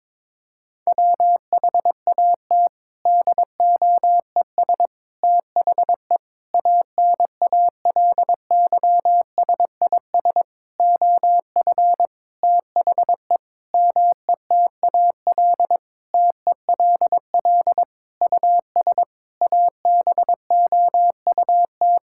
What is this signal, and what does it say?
Morse code, heard as a keyed tone, about 23 wpm, 700 Hz. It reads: WHAT DOES THE ANALYSIS OF THE METAL TELL US ABOUT
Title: Sentences from Top 1000 Words - Review Part 6 - 22wpm